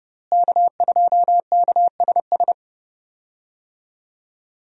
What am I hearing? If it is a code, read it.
Morse code, heard as a keyed tone, about 31 wpm, 700 Hz. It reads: K2KSS
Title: USA Call Signs - 5 Characters - Repeated 2x - 30wpm